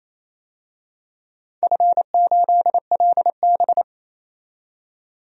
Morse code, heard as a keyed tone, about 28 wpm, 700 Hz. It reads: F8LB
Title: DXCC-100 Call Signs - 4 Characters - 28wpm